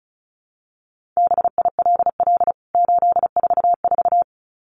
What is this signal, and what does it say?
Morse code, heard as a keyed tone, about 34 wpm, 700 Hz. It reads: BILL 844
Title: CWT Exchange 35wpm